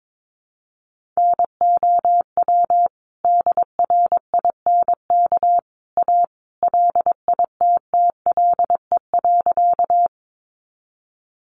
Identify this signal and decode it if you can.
Morse code, heard as a keyed tone, about 22 wpm, 700 Hz. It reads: NOW DRINK A LITTLE.
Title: Sentences from Top 1500 Words 22wpm